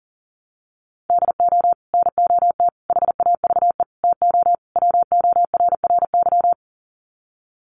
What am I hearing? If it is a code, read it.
Morse code, heard as a keyed tone, about 40 wpm, 700 Hz. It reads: DO NOT HAVE TO WORRY
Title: Sets of 5 Words 40wpm